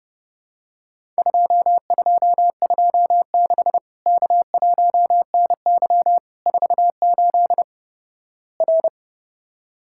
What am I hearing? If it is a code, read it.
Morse code, heard as a keyed tone, about 31 wpm, 700 Hz. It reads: 2226 K1NY 48 R
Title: Worked All Europe DX Contest - QTC Exchange - 30wpm